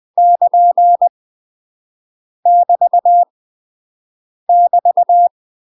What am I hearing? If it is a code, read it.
Morse code, heard as a keyed tone, about 20 wpm, 700 Hz. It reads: <KN> <BT> <BT>
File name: UK MASTER QSOs TRAINING / 20WPM / UK MASTER QSOs 0067